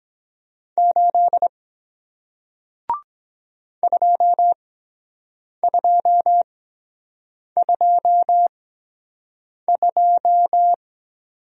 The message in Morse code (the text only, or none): 8 2 2 2 2